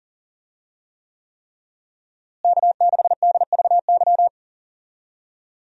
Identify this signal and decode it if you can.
Morse code, heard as a keyed tone, about 40 wpm, 700 Hz. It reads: K6DVY
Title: USA Call Signs - 5 Characters - Repeated 3x - 40wpm